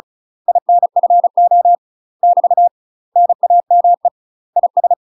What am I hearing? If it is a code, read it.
Morse code, heard as a keyed tone, about 33 wpm, 700 Hz. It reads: INFO <BT> NAME IS
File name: UK MASTER QSOs TRAINING / 35WPM / UK MASTER QSOs 0017